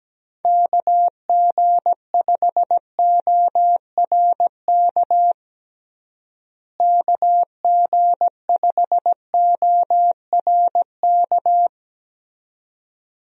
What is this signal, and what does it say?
Morse code, heard as a keyed tone, about 17 wpm, 700 Hz. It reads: KG5ORK KG5ORK
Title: USA Call Signs - 6 Characters - Repeated 2x - 17wpm